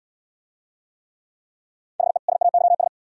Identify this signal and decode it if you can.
Morse code, heard as a keyed tone, about 73 wpm, 700 Hz. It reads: HE SAYS